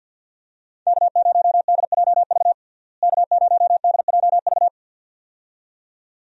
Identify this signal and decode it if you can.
Morse code, heard as a keyed tone, about 47 wpm, 700 Hz. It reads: K0DJV K0DJV
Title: USA Call Signs - 5 Characters - Repeated 2x - 50wpm